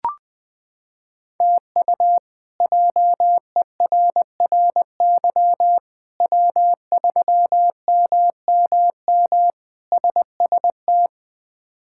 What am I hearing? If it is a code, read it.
Morse code, heard as a keyed tone, about 21 wpm, 700 Hz. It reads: TU JERRY W3MMM SST
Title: K1USN SST QSO Practice - 20wpm